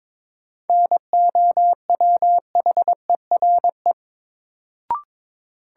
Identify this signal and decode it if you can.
Morse code, heard as a keyed tone, about 22 wpm, 700 Hz. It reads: NOWHERE
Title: Sets of 5 Words Encore - New Word Drill - 22wpm